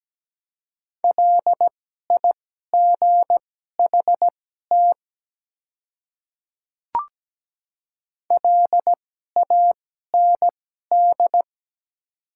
Morse code, heard as a keyed tone, about 17 wpm, spaced as for 12 wpm, 700 Hz. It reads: LIGHT LAND
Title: 2x Character Spacing - Top 200 Words - 17wpm